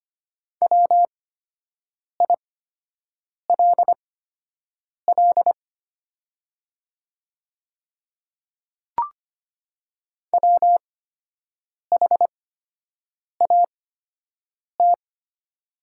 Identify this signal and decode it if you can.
Morse code, heard as a keyed tone, about 25 wpm, spaced as for 7 wpm, 700 Hz. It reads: WILL WHAT
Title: 8x Character Spacing - Top 100 Words - 25wpm